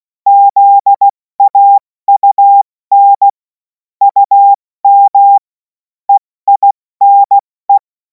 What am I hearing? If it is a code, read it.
Morse code, heard as a keyed tone, about 16 wpm, 800 Hz. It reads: ZAUN UM EINE
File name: Morsekurs / Witze / Witz 14. April 2026 70 F80